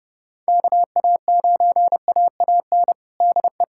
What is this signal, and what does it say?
Morse code, heard as a keyed tone, about 31 wpm, 700 Hz. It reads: KA9AAN DE
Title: USA Callsign DE Callsign 30wpm